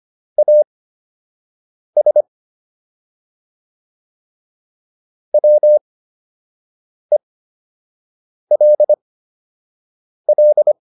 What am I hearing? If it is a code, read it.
Morse code, heard as a keyed tone, about 25 wpm, spaced as for 6 wpm, 600 Hz. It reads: AS WELL